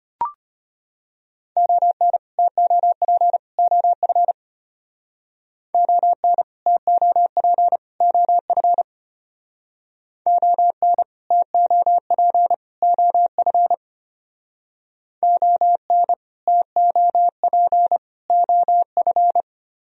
Morse code, heard as a keyed tone, about 31 wpm, 700 Hz. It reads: ON TOP OF ON TOP OF ON TOP OF ON TOP OF
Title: Speed-Racing - Sets of 3 Words 25wpm